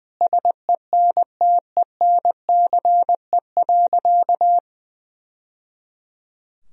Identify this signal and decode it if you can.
Morse code, heard as a keyed tone, about 20 wpm, 700 Hz. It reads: SENTENCE.
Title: Sentences from Top 1000 Words - Review Part 2 - 20wpm